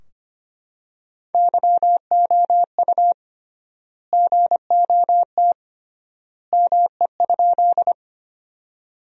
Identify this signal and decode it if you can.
Morse code, heard as a keyed tone, about 25 wpm, 700 Hz. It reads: YOU GOT ME?
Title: 3x Word-Spacing - Sentences from Top 300 Words - 25wpm